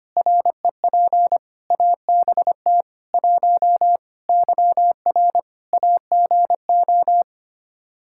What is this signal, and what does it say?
Morse code, heard as a keyed tone, about 25 wpm, 700 Hz. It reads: REP ABT 1 YR AGO
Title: Practice QSO - Exchange 3 - More Info - 25wpm